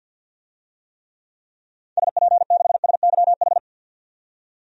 Morse code, heard as a keyed tone, about 46 wpm, 700 Hz. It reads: SP6SYH
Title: DXCC-100 Call Signs - 6 Characters - 50wpm